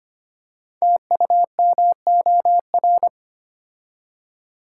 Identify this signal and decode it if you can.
Morse code, heard as a keyed tone, about 25 wpm, 700 Hz. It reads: TUMOR
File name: 5 Letter Words - First 3 Letters Given - 25wpm